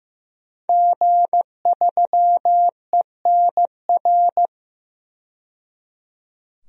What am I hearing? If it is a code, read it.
Morse code, heard as a keyed tone, about 15 wpm, 700 Hz. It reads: G3ENR
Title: DXCC-100 Call Signs - 5 Characters - 15wpm